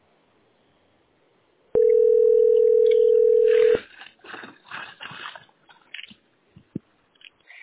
Ringback tone (North America), 440 + 480 Hz.